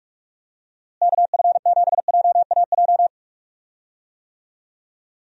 Morse code, heard as a keyed tone, about 46 wpm, 700 Hz. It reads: KF7JAJ
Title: USA Call Signs - 6 Characters - Repeated 2x - 45wpm